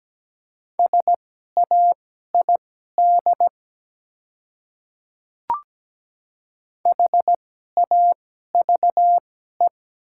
Morse code, heard as a keyed tone, about 17 wpm, spaced as for 12 wpm, 690 Hz. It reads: SAID HAVE
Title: 2x Character Spacing - Top 300 Words - 17wpm